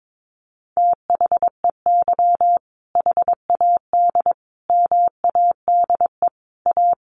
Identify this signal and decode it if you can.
Morse code, heard as a keyed tone, about 22 wpm, 700 Hz. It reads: THEY HAD MADE A
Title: Sentences from Top 900 Words 22wpm